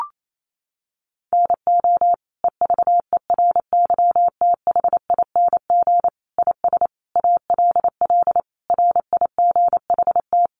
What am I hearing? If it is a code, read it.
Morse code, heard as a keyed tone, about 28 wpm, 700 Hz. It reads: NO EVERYTHING IS ALL RIGHT
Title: Sentences from Top 700 Words 28wpm